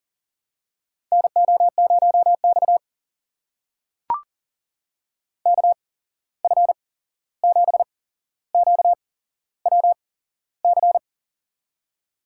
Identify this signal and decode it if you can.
Morse code, heard as a keyed tone, about 40 wpm, spaced as for 15 wpm, 700 Hz. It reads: NO0X KF7QWC
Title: USA Call Signs - 8x Character Spacing - 40wpm